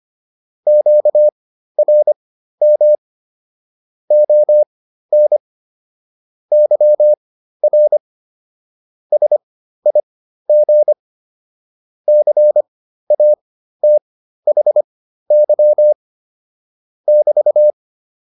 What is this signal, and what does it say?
Morse code, heard as a keyed tone, about 25 wpm, spaced as for 13 wpm, 600 Hz. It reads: QRM ON YR SIG CATHY <BT>